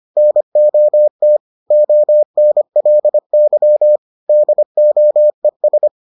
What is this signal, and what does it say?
Morse code, heard as a keyed tone, about 25 wpm, 600 Hz. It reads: NOT ONLY DOES